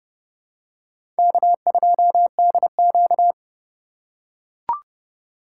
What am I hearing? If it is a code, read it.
Morse code, heard as a keyed tone, about 30 wpm, 700 Hz. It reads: K2DQ
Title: USA Call Signs - 2x Character Spacing - 30wpm